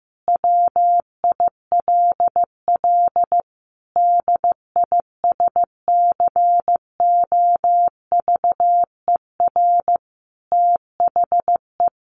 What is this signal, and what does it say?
Morse code, heard as a keyed tone, about 15 wpm, 700 Hz. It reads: WILL DISCOVER THE
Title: Sentences from Top 1100 Words 15wpm